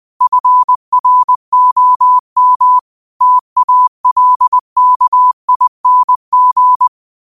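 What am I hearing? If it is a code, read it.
Morse code, heard as a keyed tone, about 20 wpm, 1000 Hz. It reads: FROM TALKING
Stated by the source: Morse code quote of the day at 20 wpm.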